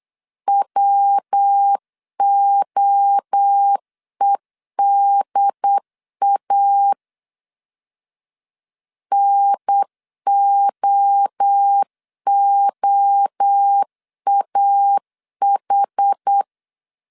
Morse code, heard as a keyed tone, about 8 wpm, 800 Hz. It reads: WOEDA NOOAH